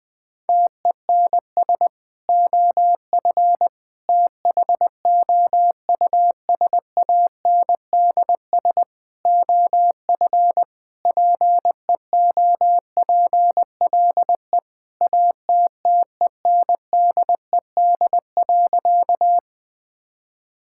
Morse code, heard as a keyed tone, about 20 wpm, 700 Hz. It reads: TENS OF THOUSANDS OF PEOPLE ATTENDED.
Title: Sentences with 4-Word Phrases - Spoken Beginning - 20wpm